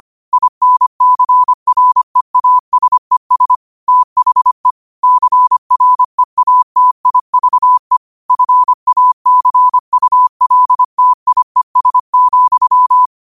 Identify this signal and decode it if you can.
Morse code, heard as a keyed tone, about 25 wpm, 1000 Hz. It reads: INCREASES THE CREATIVE FACULTIES,
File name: qotdfiles / 25wpm